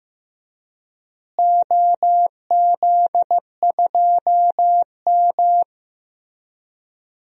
Morse code, heard as a keyed tone, about 15 wpm, 700 Hz. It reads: OZ2M
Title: DXCC-100 Call Signs - 4 Characters - Repeated 2x - 15wpm